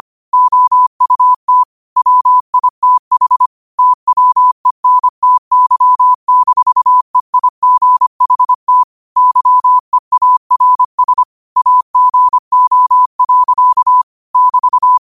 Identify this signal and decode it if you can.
Morse code, heard as a keyed tone, about 25 wpm, 1000 Hz. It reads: OUT WITH TWENTY-EIGHT YEARS AGO. <BT>
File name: qotdfiles / 25wpm